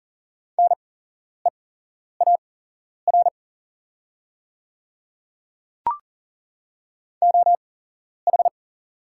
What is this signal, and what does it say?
Morse code, heard as a keyed tone, about 40 wpm, spaced as for 11 wpm, 700 Hz. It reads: NEAR OH
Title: 8x Character Spacing - Top 500 Words - 40wpm